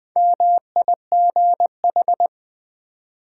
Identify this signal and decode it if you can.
Morse code, heard as a keyed tone, about 20 wpm, 690 Hz. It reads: MIGH
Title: Build A Word - 1 Letter At A Time - 5-Letter Words - 20wpm